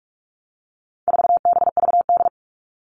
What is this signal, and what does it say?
Morse code, heard as a keyed tone, about 44 wpm, 700 Hz. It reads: 4BVD
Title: All Cut Numbers - 4-Digit 45wpm